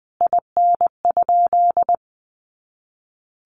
Morse code, heard as a keyed tone, about 19 wpm, 700 Hz. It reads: IN?
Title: Sentences from Top 100 Words 20wpm